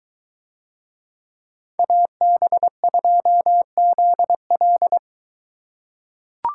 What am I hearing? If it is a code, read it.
Morse code, heard as a keyed tone, about 22 wpm, 700 Hz. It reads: AB2ZL